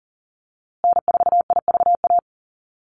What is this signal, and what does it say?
Morse code, heard as a keyed tone, about 40 wpm, 700 Hz. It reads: N4IVA
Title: USA Call Signs Repeated 2x 40wpm